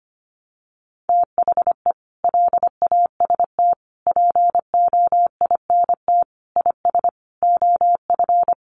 Morse code, heard as a keyed tone, about 25 wpm, 700 Hz. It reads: THE LAST POINT IS OF
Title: Sentences from Top 1200 Words 25wpm